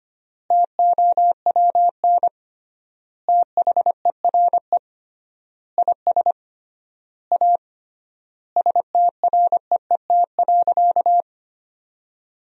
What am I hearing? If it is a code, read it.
Morse code, heard as a keyed tone, about 25 wpm, 700 Hz. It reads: TOWN THERE IS A STREET.
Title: 3x Word-Spacing - Sentences from Top 500 Words - 25wpm